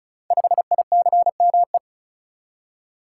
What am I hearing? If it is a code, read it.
Morse code, heard as a keyed tone, about 36 wpm, 690 Hz. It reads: 5ICME
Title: Code Groups - 5 Letter:Numbers - 35wpm